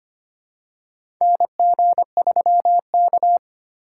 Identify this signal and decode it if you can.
Morse code, heard as a keyed tone, about 25 wpm, 700 Hz. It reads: NG3K
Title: USA Call Signs - 4 Characters - 4x Character Spacing - 25wpm